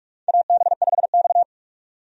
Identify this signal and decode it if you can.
Morse code, heard as a keyed tone, about 46 wpm, 700 Hz. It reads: AB5X
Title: USA Call Signs - 4 Characters - 45wpm